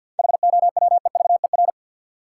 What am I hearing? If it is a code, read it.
Morse code, heard as a keyed tone, about 50 wpm, 700 Hz. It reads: HOWEVER
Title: Sentences from Top 600 Words - New Word Drill - 50wpm